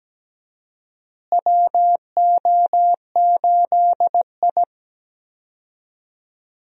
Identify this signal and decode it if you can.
Morse code, heard as a keyed tone, about 17 wpm, 700 Hz. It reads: WO8I
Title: USA Call Signs - 4 Characters - Repeated 3x - 17wpm